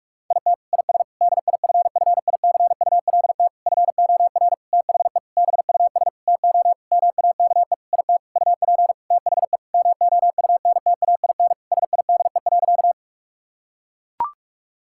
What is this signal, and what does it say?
Morse code, heard as a keyed tone, about 41 wpm, 700 Hz. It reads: IT IS DIFFICULT FOR THE BUS TO MAKE IT UP THE MOUNTAIN SIDE.
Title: Sentences from Top 1000 Words - Review Part 6 - 45wpm